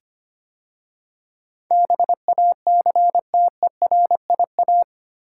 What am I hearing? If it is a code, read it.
Morse code, heard as a keyed tone, about 25 wpm, 700 Hz. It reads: BACTERIA
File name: 8 Letter Words - First 3 Letters Given - 25wpm